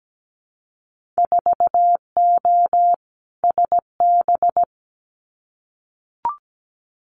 Morse code, heard as a keyed tone, about 17 wpm, 700 Hz. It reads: 4O SB
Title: Winter Field Day Exchanges 17wpm